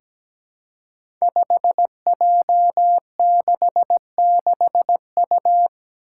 Morse code, heard as a keyed tone, about 17 wpm, 700 Hz. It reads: 5J66U
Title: Five Challenging Letter-Number Combinations 17wpm